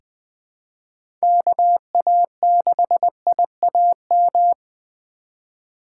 Morse code, heard as a keyed tone, about 20 wpm, 700 Hz. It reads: KA6IAM
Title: USA Call Signs - 6 Characters - 3x Character Spacing - 20wpm